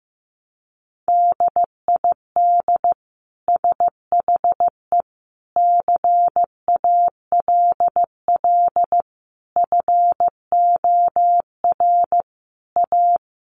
Morse code, heard as a keyed tone, about 15 wpm, 700 Hz. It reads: DID SHE CALL FOR A
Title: Sentences from Top 1300 Words 15wpm